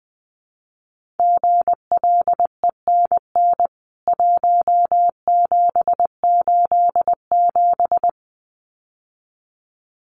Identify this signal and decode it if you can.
Morse code, heard as a keyed tone, about 20 wpm, 700 Hz. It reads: GLENN 1787
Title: CWT Exchange 20wpm